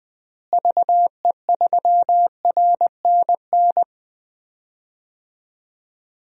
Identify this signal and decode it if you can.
Morse code, heard as a keyed tone, about 20 wpm, 700 Hz. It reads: VE3RNN
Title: DXCC-100 Call Signs - 6 Characters - Repeated 2x - 20wpm